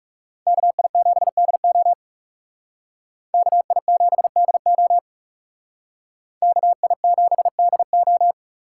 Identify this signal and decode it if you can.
Morse code, heard as a keyed tone, about 38 wpm, 700 Hz. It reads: KI7DO KI7DO KI7DO